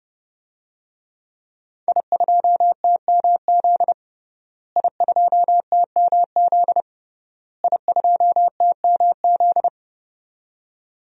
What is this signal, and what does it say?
Morse code, heard as a keyed tone, about 30 wpm, 700 Hz. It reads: I2TMZ I2TMZ I2TMZ
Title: DXCC-100 Call Signs - 5 Characters - Repeated 3x - 30wpm